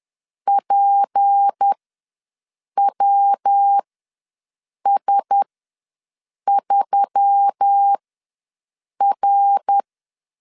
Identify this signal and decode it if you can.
Morse code, heard as a keyed tone, about 10 wpm, spaced as for 6 wpm, 800 Hz. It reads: PWS3R
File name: LEZIONI / 6 PKX0 / 6 P ws 10wpm 3751